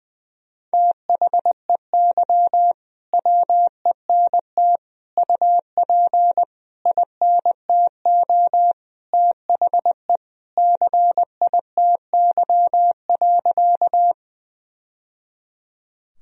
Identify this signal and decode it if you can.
Morse code, heard as a keyed tone, about 21 wpm, 700 Hz. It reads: THEY WENT UP INTO THE CITY.
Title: Sentences from Top 1000 Words - Review Part 1 - 20wpm